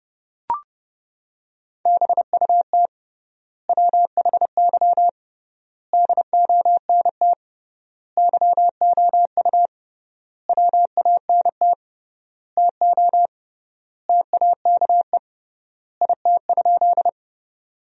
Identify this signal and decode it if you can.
Morse code, heard as a keyed tone, about 30 wpm, 700 Hz. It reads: BUT WHY DONT YOU WANT TO TAKE IT?
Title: 3x Word-Spacing - Sentences from Top 300 Words - 30wpm